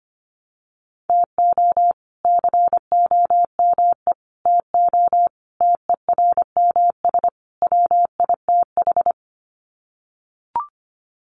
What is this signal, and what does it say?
Morse code, heard as a keyed tone, about 25 wpm, 700 Hz. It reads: TO COME TO TERMS WITH
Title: Sets of 5 Words Encore 25wpm